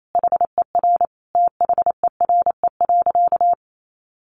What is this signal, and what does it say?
Morse code, heard as a keyed tone, about 28 wpm, 700 Hz. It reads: HER THERE.
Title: Sentences from Top 1500 Words 28wpm